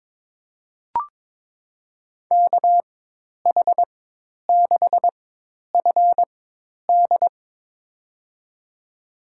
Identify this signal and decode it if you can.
Morse code, heard as a keyed tone, about 22 wpm, spaced as for 10 wpm, 700 Hz. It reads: KH6FD
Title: USA Call Signs - 4x Character Spacing - 22wpm